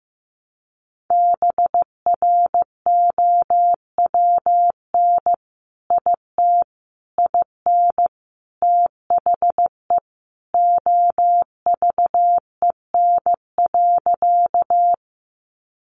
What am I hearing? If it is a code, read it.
Morse code, heard as a keyed tone, about 15 wpm, 700 Hz. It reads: BROWN IT IN THE OVEN.
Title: Sentences from Top 1200 Words 15wpm